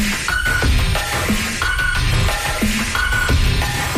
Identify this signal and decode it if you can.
Emergency Alert System (SAME) tones, heard as harsh buzzing bursts. It is heard from 0.2 to 3.3 s.
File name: wrct-log hi- 02